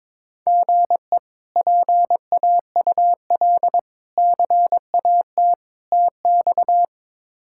Morse code, heard as a keyed tone, about 22 wpm, 700 Hz. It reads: GE PAUL CAT TX
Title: K1USN SST QSO Practice - 22wpm